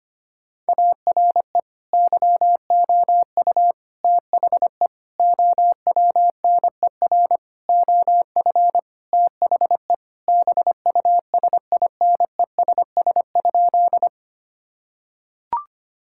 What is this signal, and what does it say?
Morse code, heard as a keyed tone, about 25 wpm, 700 Hz. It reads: ARE YOU THE OWNER OF THE BUSINESS?
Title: Sentences from Top 1600 Words 25wpm